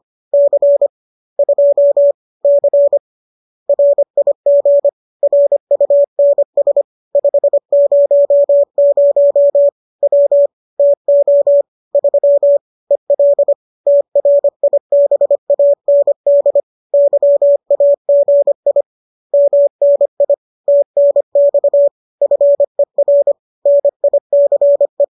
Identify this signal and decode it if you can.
Morse code, heard as a keyed tone, about 25 wpm, 570 Hz. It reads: C 2 C RIG RUNS 500 W TO 3 EL TRIBAND YAGI MNI TNX FER NICE
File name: qso219 25